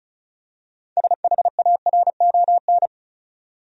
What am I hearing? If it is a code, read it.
Morse code, heard as a keyed tone, about 36 wpm, 700 Hz. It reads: SHARON
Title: USA Common Names - 35wpm